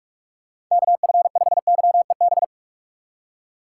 Morse code, heard as a keyed tone, about 46 wpm, 700 Hz. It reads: KF5YEB
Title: USA Call Signs - 6 Characters - 2x Character Spacing - 45wpm